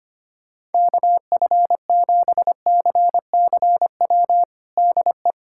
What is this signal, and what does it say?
Morse code, heard as a keyed tone, about 25 wpm, 700 Hz. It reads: KF7CCW DE
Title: USA Callsign DE Callsign - 6 Characters - 25wpm